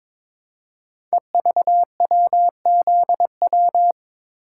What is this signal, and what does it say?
Morse code, heard as a keyed tone, about 22 wpm, 700 Hz. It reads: EVWZW
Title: Code Groups - 5 Letter:Numbers - 22wpm